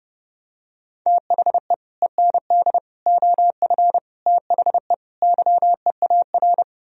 Morse code, heard as a keyed tone, about 29 wpm, 700 Hz. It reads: THE END OF THE YEAR